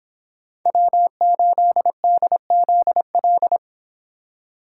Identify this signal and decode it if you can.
Morse code, heard as a keyed tone, about 26 wpm, 700 Hz. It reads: W8DZL
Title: Speed Racing - USA 1x3 Call Signs 22wpm